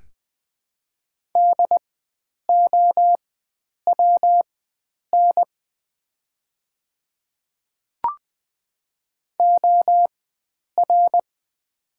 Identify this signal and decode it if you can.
Morse code, heard as a keyed tone, about 21 wpm, spaced as for 9 wpm, 700 Hz. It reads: DOWN OR
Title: 4x Character Spacing - Top 100 Words - 20wpm